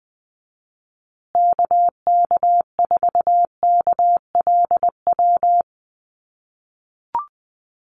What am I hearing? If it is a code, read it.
Morse code, heard as a keyed tone, about 20 wpm, 700 Hz. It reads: KK4KLW
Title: USA Call Signs Repeated 3x 20wpm